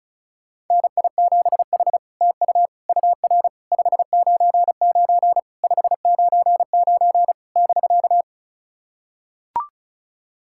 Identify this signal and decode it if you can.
Morse code, heard as a keyed tone, about 33 wpm, 700 Hz. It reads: NI7H TU UR 599 599 <BK>
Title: SOTA QSO Practice 35wpm